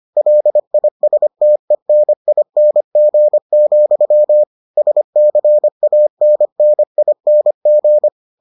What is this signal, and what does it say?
Morse code, heard as a keyed tone, about 25 wpm, 600 Hz. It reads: LISTENING, SCANNING